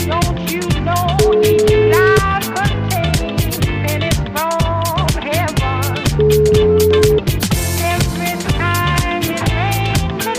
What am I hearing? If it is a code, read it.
Ringback tone (Europe, 425 Hz), with noise.